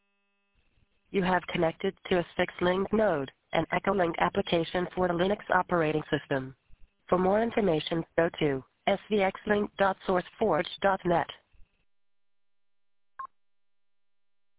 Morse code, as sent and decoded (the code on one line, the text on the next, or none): none